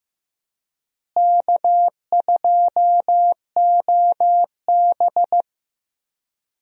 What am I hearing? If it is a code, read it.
Morse code, heard as a keyed tone, about 15 wpm, 700 Hz. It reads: K2OB
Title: USA Call Signs - 4 Characters - Repeated 2x - 15wpm